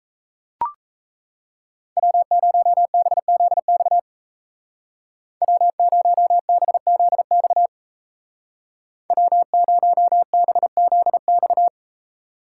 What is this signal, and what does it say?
Morse code, heard as a keyed tone, about 37 wpm, 700 Hz. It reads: W0BZX W0BZX W0BZX